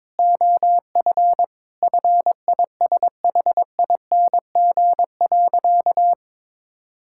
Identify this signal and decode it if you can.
Morse code, heard as a keyed tone, about 22 wpm, 700 Hz. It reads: OF FISHING.
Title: Sentences from Top 1000 Words - Review Part 5 - 22wpm